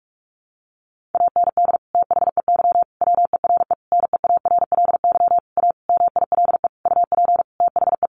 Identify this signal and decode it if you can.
Morse code, heard as a keyed tone, about 44 wpm, 700 Hz. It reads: AND THEY WERE NEARLY A MILE UP THE
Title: Sentences from Top 1300 Words 45wpm